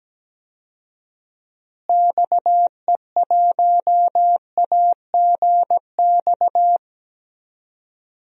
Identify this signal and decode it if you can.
Morse code, heard as a keyed tone, about 17 wpm, 700 Hz. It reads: XE1AGX
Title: DXCC-100 Call Signs - 6 Characters - Repeated 2x - 17wpm